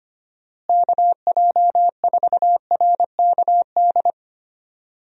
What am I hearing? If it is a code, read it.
Morse code, heard as a keyed tone, about 25 wpm, 700 Hz. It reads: KJ4RKD
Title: USA Call Signs - 6 Characters - Repeated 3x - 25wpm